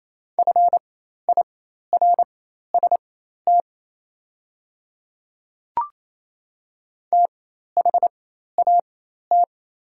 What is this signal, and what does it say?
Morse code, heard as a keyed tone, about 28 wpm, spaced as for 13 wpm, 700 Hz. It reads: FIRST THAT
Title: 4x Character Spacing - Top 100 Words - 28wpm